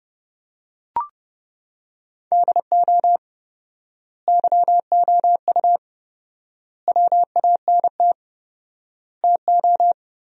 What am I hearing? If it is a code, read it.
Morse code, heard as a keyed tone, about 32 wpm, 700 Hz. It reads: DO YOU WANT TO
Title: 4x Word-Spacing - Sentences from Top 300 Words - 30wpm